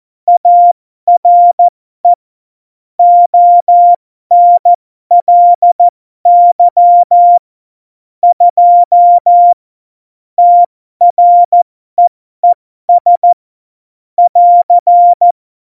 Morse code, heard as a keyed tone, about 14 wpm, spaced as for 12 wpm, 700 Hz. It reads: ARE ONLY 2 TREES <AR>
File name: Morsekurs / Witze